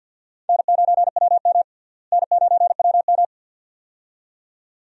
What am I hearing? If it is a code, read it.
Morse code, heard as a keyed tone, about 48 wpm, 700 Hz. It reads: N9WM N9WM
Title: USA Call Signs - 4 Characters - Repeated 2x - 50wpm